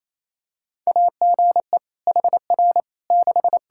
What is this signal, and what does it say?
Morse code, heard as a keyed tone, about 28 wpm, 700 Hz. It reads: AGE HR 6
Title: Practice QSO - Exchange 3 - More Info - 28wpm